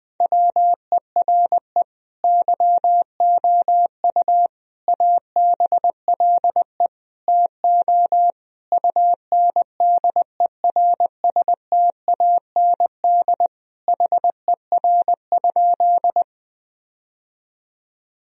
Morse code, heard as a keyed tone, about 20 wpm, 700 Hz. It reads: WERE YOU ABLE TO UNDERSTAND HER?
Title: Sentences from Top 1000 Words - Review Part 2 - 20wpm